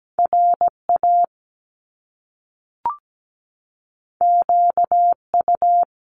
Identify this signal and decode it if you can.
Morse code, heard as a keyed tone, about 17 wpm, 700 Hz. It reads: RA QU
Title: Top Two Letter Combinations in Words 17wpm